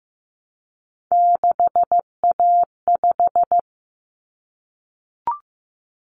Morse code, heard as a keyed tone, about 15 wpm, 700 Hz. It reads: 6A5
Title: Common Cut Numbers - 3-Digit 15wpm